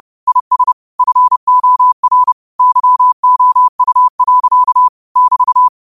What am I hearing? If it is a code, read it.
Morse code, heard as a keyed tone, about 30 wpm, 1000 Hz. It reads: IS FOR YOU. <BT>
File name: qotdfiles / 30wpm